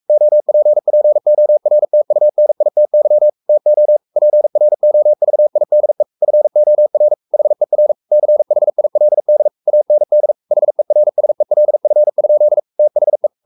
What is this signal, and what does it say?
Morse code, heard as a keyed tone, about 44 wpm, 600 Hz. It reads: OPPORTUNITY TO PROVIDE FOR HER CHILD AND HERSELF? THE
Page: Romanen Never af Ken Follet på engelsk.